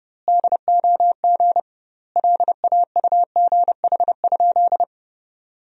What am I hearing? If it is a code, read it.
Morse code, heard as a keyed tone, about 30 wpm, 700 Hz. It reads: DOG LAUGH?
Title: 2x Word-Spacing - Sentences from Top 500 Words - 30wpm